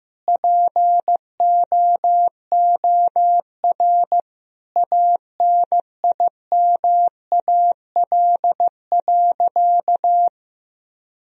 Morse code, heard as a keyed tone, about 15 wpm, 700 Hz. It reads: POOR ANIMAL.
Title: Sentences from Top 1000 Words - Review Part 1 - 15wpm